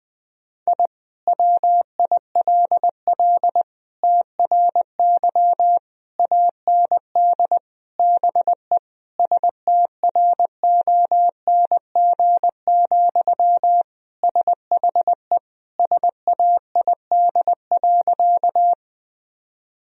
Morse code, heard as a keyed tone, about 21 wpm, 700 Hz. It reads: I WILL TRY AND BE STRONG, SHE SAID.
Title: Sentences from Top 1000 Words - Review Part 3 - 20wpm